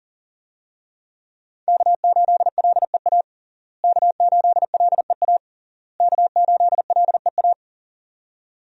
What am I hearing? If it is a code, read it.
Morse code, heard as a keyed tone, about 40 wpm, 700 Hz. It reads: K8LEA K8LEA K8LEA
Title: USA Call Signs - 5 Characters - Repeated 3x - 40wpm